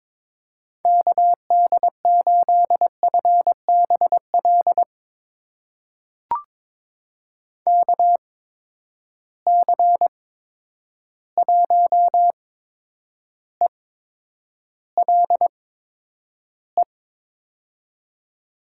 Morse code, heard as a keyed tone, about 22 wpm, spaced as for 9 wpm, 700 Hz. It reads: KD8FBL KC1ELE
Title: USA Call Signs - 6 Characters - 8x Character Spacing - 22wpm